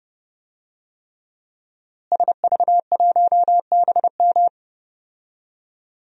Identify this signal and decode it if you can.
Morse code, heard as a keyed tone, about 30 wpm, 700 Hz. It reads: SV1BM
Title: DXCC-100 Call Signs - 5 Characters - Repeated 3x - 30wpm